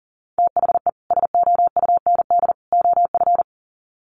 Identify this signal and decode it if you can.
Morse code, heard as a keyed tone, about 40 wpm, 700 Hz. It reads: THE SOUND OF
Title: Sets of 3 Words 40wpm